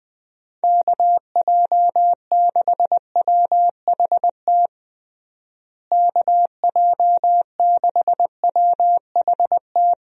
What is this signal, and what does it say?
Morse code, heard as a keyed tone, about 20 wpm, 700 Hz. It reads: KJ6WHT KJ6WHT
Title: USA Call Signs - 6 Characters - Repeated 3x - 20wpm